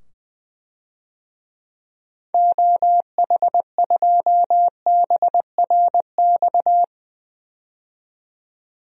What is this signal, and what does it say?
Morse code, heard as a keyed tone, about 20 wpm, 700 Hz. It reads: OH2BRX
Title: DXCC-100 Call Signs - 6 Characters - 20wpm